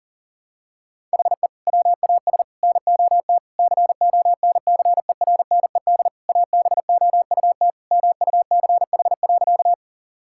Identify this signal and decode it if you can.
Morse code, heard as a keyed tone, about 40 wpm, 700 Hz. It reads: HE WAS NOT CONCERNED ABOUT MUCH.
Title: Sentences from Top 1000 Words - Review Part 6 - 40wpm